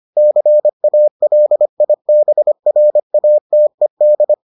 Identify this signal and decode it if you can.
Morse code, heard as a keyed tone, about 25 wpm, 600 Hz. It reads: CALIBRATED